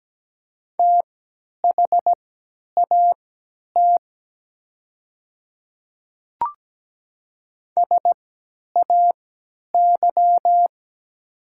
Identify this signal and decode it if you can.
Morse code, heard as a keyed tone, about 17 wpm, spaced as for 10 wpm, 690 Hz. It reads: THAT SAY
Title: 3x Character Spacing - Top 200 Words - 17wpm